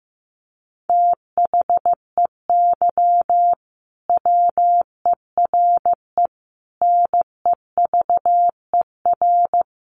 Morse code, heard as a keyed tone, about 15 wpm, 700 Hz. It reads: THEY WERE NEVER
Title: Sentences from Top 1400 Words 15wpm